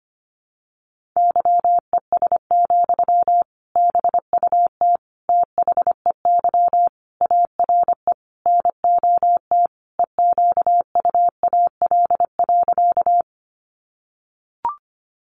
Morse code, heard as a keyed tone, about 25 wpm, 700 Hz. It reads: YES, BUT THEY ARE NOT EQUAL.
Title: Sentences from Top 1400 Words 25wpm